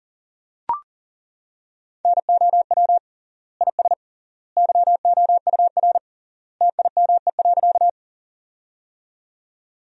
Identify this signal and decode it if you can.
Morse code, heard as a keyed tone, about 40 wpm, 700 Hz. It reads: NOW IS YOUR TIME.
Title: 3x Word-Spacing - Sentences from Top 100 Words - 40wpm